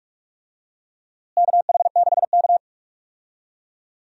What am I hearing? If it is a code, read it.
Morse code, heard as a keyed tone, about 46 wpm, 700 Hz. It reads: KH6K
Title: DXCC-100 Call Signs - 4 Characters - Repeated 3x - 45wpm